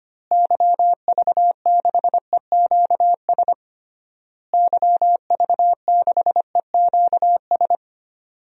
Morse code, heard as a keyed tone, about 25 wpm, 700 Hz. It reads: YV6EQS YV6EQS
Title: DXCC-100 Call Signs - 6 Characters - Repeated 3x - 25wpm